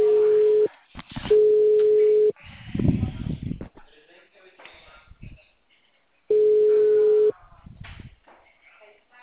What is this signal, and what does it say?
Ringback tone (Europe, 425 Hz), with speech.